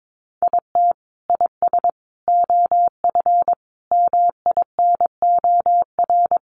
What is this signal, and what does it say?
Morse code, heard as a keyed tone, about 22 wpm, 700 Hz. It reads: IT IS OF MINOR
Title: Sentences from Top 1500 Words 22wpm